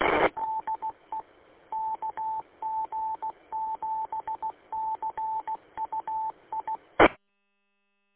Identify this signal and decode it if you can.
Morse code, heard as a keyed tone, about 16 wpm, 880 Hz. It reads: DE KG7CUI
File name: w8cmn-vhf-combined